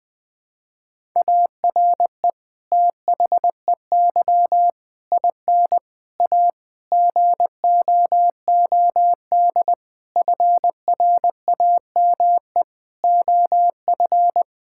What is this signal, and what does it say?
Morse code, heard as a keyed tone, about 20 wpm, 700 Hz. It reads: ARE THEY IN A GOOD FRAME OF
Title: Sentences from Top 1000 Words - Review Part 6 - 20wpm